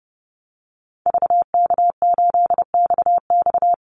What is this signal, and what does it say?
Morse code, heard as a keyed tone, about 30 wpm, 700 Hz. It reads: VK8XX
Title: Call Signs - 30wpm